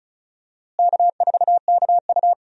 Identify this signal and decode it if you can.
Morse code, heard as a keyed tone, about 36 wpm, 700 Hz. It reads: K4KU
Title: USA Call Signs - 4 Characters - Repeated 3x - 35wpm